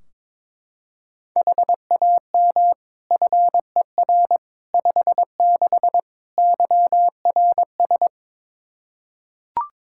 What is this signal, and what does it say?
Morse code, heard as a keyed tone, about 22 wpm, 700 Hz. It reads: HAM FER 56 YRS
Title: Practice QSO - Exchange 3 - More Info - 22wpm